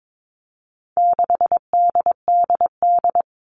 Morse code, heard as a keyed tone, about 22 wpm, 700 Hz. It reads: 6DDD
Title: All Cut Numbers - 4-Digit 22wpm